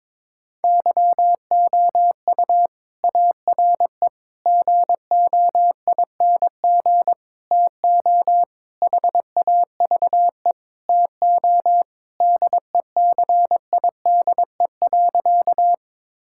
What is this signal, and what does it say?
Morse code, heard as a keyed tone, about 22 wpm, 700 Hz. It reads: YOU ARE GOING TO HAVE TO DECIDE.
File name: practice / 22wpm / Sentences with 4-Word Phrases - 22wpm